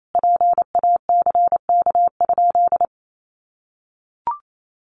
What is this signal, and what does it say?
Morse code, heard as a keyed tone, about 28 wpm, 700 Hz. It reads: PACK?
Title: Sentences from Top 1300 Words 28wpm